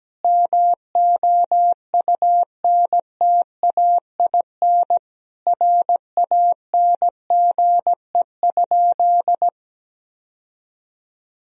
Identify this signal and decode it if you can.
Morse code, heard as a keyed tone, about 17 wpm, 700 Hz. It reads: MOUNTAIN RANGE?
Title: Sentences from Top 1000 Words - Review Part 4 - 17wpm